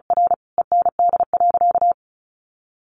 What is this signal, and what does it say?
Morse code, heard as a keyed tone, about 34 wpm, 700 Hz. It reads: R END.
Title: Easy Phrases 35wpm